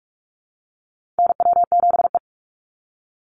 Morse code, heard as a keyed tone, about 46 wpm, 690 Hz. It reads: NW7E
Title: NA SOTA Chaser Callsigns 45wpm